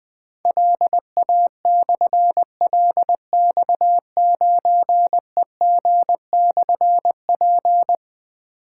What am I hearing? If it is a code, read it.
Morse code, heard as a keyed tone, about 20 wpm, 700 Hz. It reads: LA/LX9EG/P
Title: Difficult Call Signs 20wpm